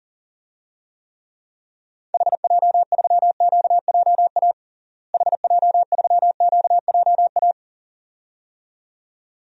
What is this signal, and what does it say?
Morse code, heard as a keyed tone, about 40 wpm, 700 Hz. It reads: HJ3QJA HJ3QJA
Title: DXCC-100 Call Signs - 6 Characters - Repeated 2x - 40wpm